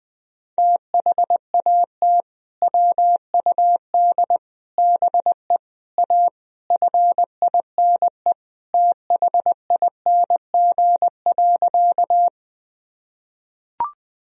Morse code, heard as a keyed tone, about 21 wpm, 700 Hz. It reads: THAT WUD BE A FINE THING.
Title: Sentences from Top 500 Words and Abbreviations 20wpm